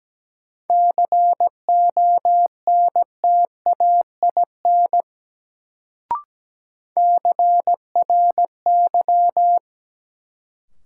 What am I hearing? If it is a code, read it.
Morse code, heard as a keyed tone, about 17 wpm, 700 Hz. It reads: CONTAIN CRY
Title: Sentences from Top 500 Words - New Word Drill - 17wpm